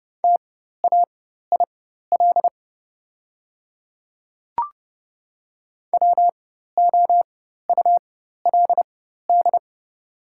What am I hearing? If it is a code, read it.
Morse code, heard as a keyed tone, about 30 wpm, spaced as for 14 wpm, 700 Hz. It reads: TAIL WOULD
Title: 4x Character Spacing - Top 500 Words - 30wpm